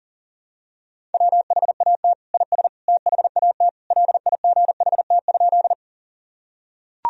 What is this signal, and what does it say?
Morse code, heard as a keyed tone, about 40 wpm, 700 Hz. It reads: WHAT IS THAT LIGHT?
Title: Sentences from Top 1000 Words - Review Part 1 - 40wpm